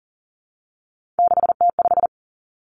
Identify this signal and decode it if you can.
Morse code, heard as a keyed tone, about 42 wpm, 700 Hz. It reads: 6T5
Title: Three Challenging Letter-Number Combinations 40wpm